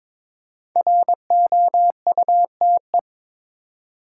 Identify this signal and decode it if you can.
Morse code, heard as a keyed tone, about 22 wpm, 700 Hz. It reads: ROUTE
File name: 5 Letter Words - First 3 Letters Given - 22wpm